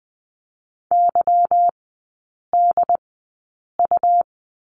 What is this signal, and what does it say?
Morse code, heard as a keyed tone, about 20 wpm, spaced as for 8 wpm, 700 Hz. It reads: YDU
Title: 166 Lesson 27e - Number 7 - Three Characters 20wpm